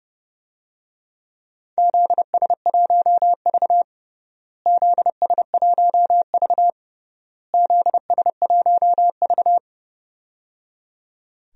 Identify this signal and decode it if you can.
Morse code, heard as a keyed tone, about 30 wpm, 700 Hz. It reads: ZS1V ZS1V ZS1V
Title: DXCC-100 Call Signs - 4 Characters - Repeated 3x - 30wpm